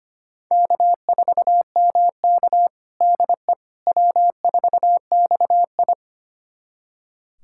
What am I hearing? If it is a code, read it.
Morse code, heard as a keyed tone, about 25 wpm, 700 Hz. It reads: K4MK DE W4XI
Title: USA Callsign DE Callsign - 4 Characters - 25wpm